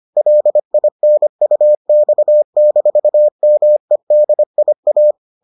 Morse code, heard as a keyed tone, about 25 wpm, 600 Hz. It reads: LINUX-MEDIA